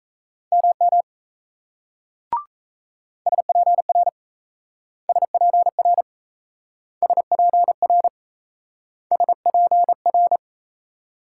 Morse code, heard as a keyed tone, about 35 wpm, 700 Hz. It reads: MM SPR SPR SPR SPR